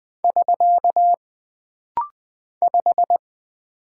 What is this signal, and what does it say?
Morse code, heard as a keyed tone, about 20 wpm, 700 Hz. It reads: <SK> 5
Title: Single Character (Letter:Number:Punctuation:Prosign) - Rapid-Fire - 20wpm